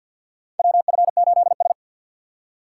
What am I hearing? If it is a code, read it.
Morse code, heard as a keyed tone, about 53 wpm, 700 Hz. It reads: WF8S
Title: USA Call Signs - 4 Characters - 3x Character Spacing - 50wpm